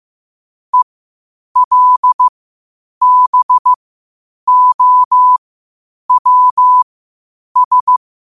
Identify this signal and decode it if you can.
Morse code, heard as a keyed tone, about 15 wpm, spaced as for 8 wpm, 1000 Hz. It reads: ELBOWS
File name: qotdfiles / 8wpm